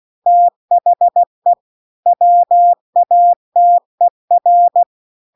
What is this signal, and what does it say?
Morse code, heard as a keyed tone, about 16 wpm, 700 Hz. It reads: THE WATER
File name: CW-The-Big-Twoheaded-River-2-16WPM